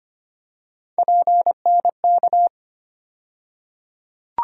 Morse code, heard as a keyed tone, about 25 wpm, 700 Hz. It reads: PNK
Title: USA Call Sign Postfixes - 3 Letters 25wpm